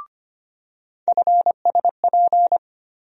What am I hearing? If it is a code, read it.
Morse code, heard as a keyed tone, about 25 wpm, 700 Hz. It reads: FSP
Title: USA Call Sign Postfixes - 3 Letters 25wpm